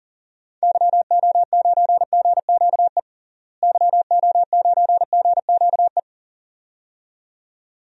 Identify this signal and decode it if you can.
Morse code, heard as a keyed tone, about 40 wpm, 700 Hz. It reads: YO9GQE YO9GQE
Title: DXCC-100 Call Signs - 6 Characters - Repeated 2x - 40wpm